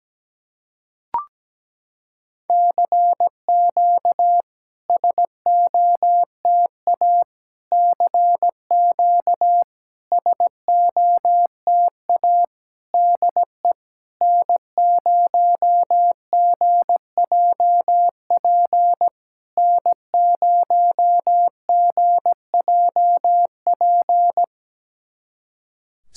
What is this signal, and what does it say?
Morse code, heard as a keyed tone, about 17 wpm, 700 Hz. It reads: CQ SOTA CQ SOTA DE N0GJP N0GJP
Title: SOTA QSO Practice 17wpm